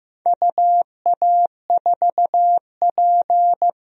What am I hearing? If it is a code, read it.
Morse code, heard as a keyed tone, about 15 wpm, 700 Hz. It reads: UA4P
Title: DXCC-100 Call Signs - 4 Characters - 15wpm